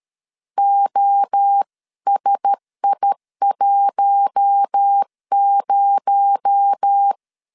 Morse code, heard as a keyed tone, about 13 wpm, 800 Hz. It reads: OSI10
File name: CW / DET / 1 6 1500C S B 12wpm 3431